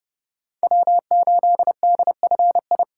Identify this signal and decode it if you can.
Morse code, heard as a keyed tone, about 32 wpm, 700 Hz. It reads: W8DFI
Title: USA Call Signs - 5 Characters - 4x Character Spacing - 30wpm